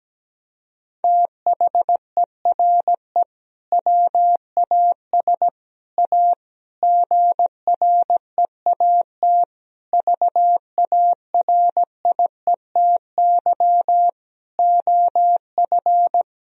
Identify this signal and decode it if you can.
Morse code, heard as a keyed tone, about 17 wpm, 700 Hz. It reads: THERE WAS A GREAT VARIETY OF
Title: Sentences from Top 1000 Words - Review Part 4 - 17wpm